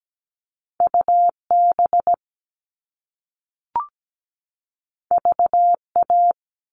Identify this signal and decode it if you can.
Morse code, heard as a keyed tone, about 17 wpm, 700 Hz. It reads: UB VA
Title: All Cut Numbers - 2-Digit 17wpm